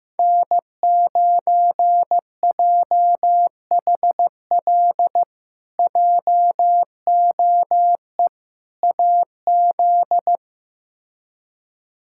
Morse code, heard as a keyed tone, about 16 wpm, 700 Hz. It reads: N9JHL JOE AZ
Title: K1USN SST QSO Practice - 15wpm